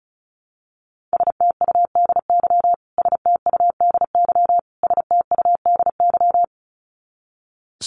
Morse code, heard as a keyed tone, about 34 wpm, 700 Hz. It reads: STUDY STUDY STUDY
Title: Top 300 Words Repeated 35wpm